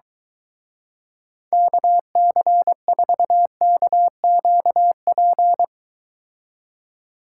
Morse code, heard as a keyed tone, about 24 wpm, 700 Hz. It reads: KC4KQP